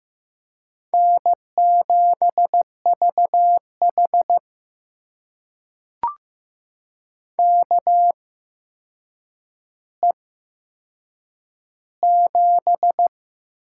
Morse code, heard as a keyed tone, about 15 wpm, spaced as for 7 wpm, 700 Hz. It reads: N7VH KE7
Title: USA Call Signs - 8x Character Spacing - 15wpm